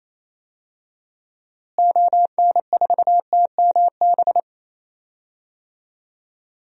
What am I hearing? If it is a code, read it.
Morse code, heard as a keyed tone, about 28 wpm, 700 Hz. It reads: ON4TMB
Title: DXCC-100 Call Signs - 6 Characters - Repeated 2x - 28wpm